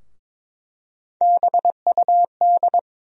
Morse code, heard as a keyed tone, about 22 wpm, 700 Hz. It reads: BUD
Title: USA Call Sign Postfixes - 3 Letters 22wpm